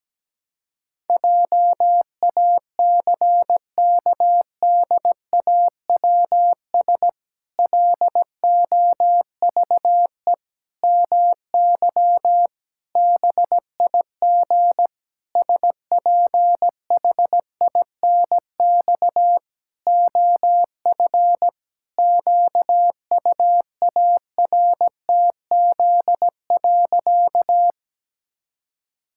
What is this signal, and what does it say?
Morse code, heard as a keyed tone, about 17 wpm, 700 Hz. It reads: JACKDAWS LOVE MY BIG SPHINX OF QUARTZ.
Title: Pangram Challenge v2 - 17wpm